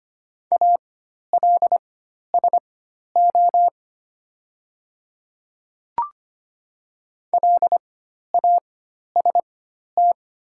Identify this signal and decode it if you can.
Morse code, heard as a keyed tone, about 25 wpm, spaced as for 12 wpm, 700 Hz. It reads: ALSO LAST
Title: 4x Character Spacing - Top 300 Words - 25wpm